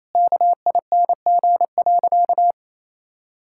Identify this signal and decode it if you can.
Morse code, heard as a keyed tone, about 27 wpm, 700 Hz. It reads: KING.
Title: Sentences from Top 1000 Words - Review Part 6 - 28wpm